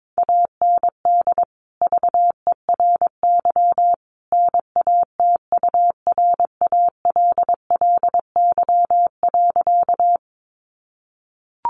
Morse code, heard as a keyed tone, about 22 wpm, 700 Hz. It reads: AND VERY NATURALLY.
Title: Sentences from Top 1300 Words 22wpm